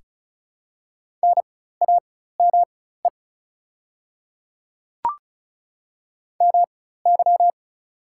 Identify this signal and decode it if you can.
Morse code, heard as a keyed tone, about 34 wpm, spaced as for 16 wpm, 700 Hz. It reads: NAME MY
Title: 4x Character Spacing - Top 200 Words - 35wpm